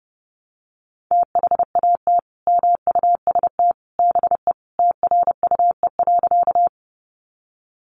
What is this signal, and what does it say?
Morse code, heard as a keyed tone, about 30 wpm, 700 Hz. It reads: THAT MUST BE TRUE.
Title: Sentences from Top 400 Words 30wpm